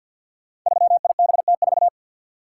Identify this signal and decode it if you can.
Morse code, heard as a keyed tone, about 52 wpm, 700 Hz. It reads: 3IBT4
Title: Code Groups - 5 Letter:Numbers - 50wpm